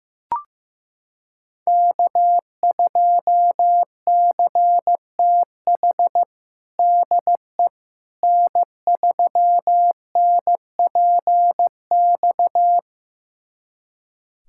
Morse code, heard as a keyed tone, about 15 wpm, 700 Hz. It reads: K2CTH DE N3NPX
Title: USA Callsign DE Callsign - 5 Characters - 15wpm